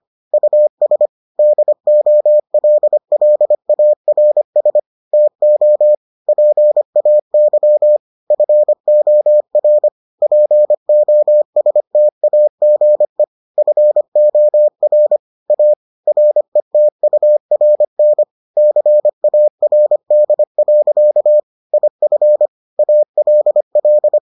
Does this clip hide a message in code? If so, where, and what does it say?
Yes: Morse code from 0.3 to 24.2 s, heard as a keyed tone, about 25 wpm, 600 Hz. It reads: US DOLLARS TO PAY FOR POSTAGE FOR A RETURN CARD. IF ALL